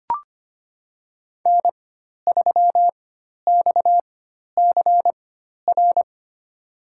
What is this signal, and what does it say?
Morse code, heard as a keyed tone, about 25 wpm, spaced as for 12 wpm, 700 Hz. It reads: N3XCR
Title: USA Call Signs - 5 Characters - 4x Character Spacing - 25wpm